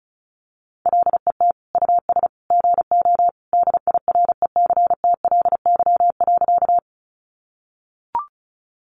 Morse code, heard as a keyed tone, about 34 wpm, 700 Hz. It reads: LET US GO DIRECTLY.
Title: Sentences from Top 800 Words 35wpm